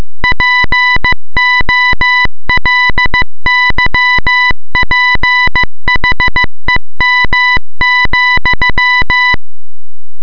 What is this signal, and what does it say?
Morse code, heard as a keyed tone, about 15 wpm, 2000 Hz. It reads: POLYPHEM,